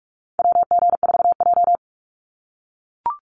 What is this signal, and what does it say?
Morse code, heard as a keyed tone, about 44 wpm, 700 Hz. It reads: WG4J
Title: USA Call Signs 45wpm